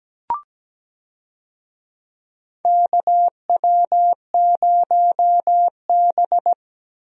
Morse code, heard as a keyed tone, about 17 wpm, 700 Hz. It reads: KW0B
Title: USA Call Signs - 4 Characters - Repeated 3x - 17wpm